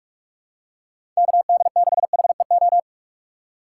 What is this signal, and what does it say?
Morse code, heard as a keyed tone, about 47 wpm, 700 Hz. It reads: KD6HEO
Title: USA Call Signs - 6 Characters - 6x Character Spacing - 45wpm